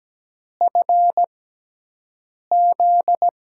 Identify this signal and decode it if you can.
Morse code, heard as a keyed tone, about 17 wpm, spaced as for 6 wpm, 700 Hz. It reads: FZ
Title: USA Call Signs - 5 Characters - 6x Character Spacing - 17wpm